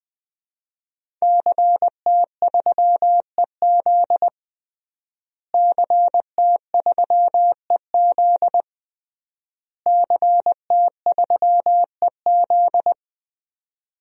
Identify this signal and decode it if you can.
Morse code, heard as a keyed tone, about 20 wpm, 700 Hz. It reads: CT3EZ CT3EZ CT3EZ
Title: DXCC-100 Call Signs - 5 Characters - Repeated 3x - 20wpm